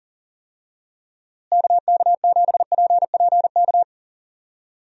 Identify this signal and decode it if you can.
Morse code, heard as a keyed tone, about 40 wpm, 700 Hz. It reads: KK7PPK
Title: USA Call Signs - 6 Characters - Repeated 2x - 40wpm